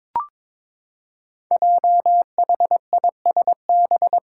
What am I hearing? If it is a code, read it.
Morse code, heard as a keyed tone, about 23 wpm, 700 Hz. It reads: JHISB
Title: Five Challenging Letter-Number Combinations 22wpm